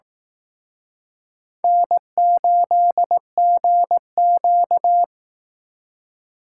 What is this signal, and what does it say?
Morse code, heard as a keyed tone, about 20 wpm, 700 Hz. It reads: N8GQ